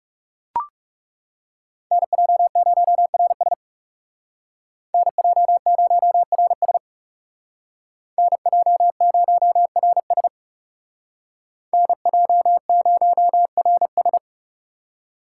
Morse code, heard as a keyed tone, about 35 wpm, 700 Hz. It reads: NJ0RS NJ0RS NJ0RS NJ0RS
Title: Speed Racing - USA 2x2 Call Signs 30wpm